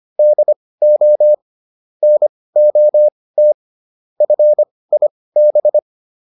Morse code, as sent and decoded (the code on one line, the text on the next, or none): -.. --- / -. --- - / ..-. .. -...
DO NOT FIB